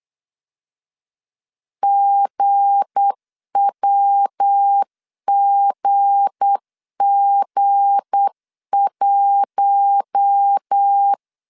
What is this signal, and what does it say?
Morse code, heard as a keyed tone, about 8 wpm, 800 Hz. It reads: GWGG1
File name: LEZIONI / 7 GF9Z / 7 G 8wpm 3118